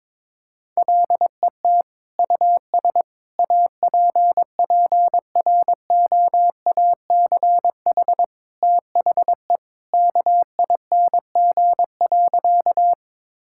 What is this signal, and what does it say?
Morse code, heard as a keyed tone, about 21 wpm, 700 Hz. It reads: LET US APPROACH THE KING.
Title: Sentences from Top 1000 Words - Review Part 6 - 22wpm